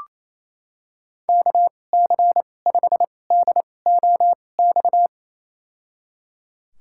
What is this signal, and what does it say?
Morse code, heard as a keyed tone, about 28 wpm, spaced as for 20 wpm, 700 Hz. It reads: KC5DOX
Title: USA Call Signs - 6 Characters - 2x Character Spacing - 28wpm